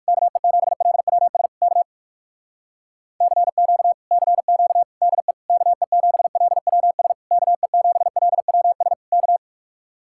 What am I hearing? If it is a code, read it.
Morse code, heard as a keyed tone, about 58 wpm, 700 Hz. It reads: KE7LWS K CQ CQ DE KE7LWS KE7LWS K